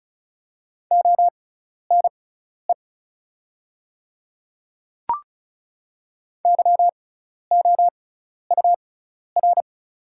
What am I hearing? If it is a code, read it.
Morse code, heard as a keyed tone, about 36 wpm, spaced as for 12 wpm, 700 Hz. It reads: ONE YOUR
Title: 6x Character Spacing - Top 100 Words - 35wpm